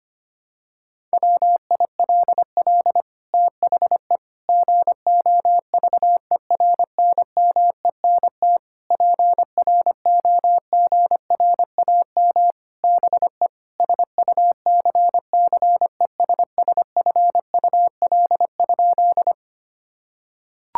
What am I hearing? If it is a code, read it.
Morse code, heard as a keyed tone, about 25 wpm, 700 Hz. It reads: WILL THE GOVERNMENT PROGRAM BE SUCCESSFUL?
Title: Sentences from Top 1000 Words - Review Part 5 - 25wpm